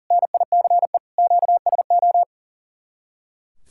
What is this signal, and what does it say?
Morse code, heard as a keyed tone, about 39 wpm, 700 Hz. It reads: NICE QSO
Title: Speed-Racing Common QSO Phrases Encore - 40wpm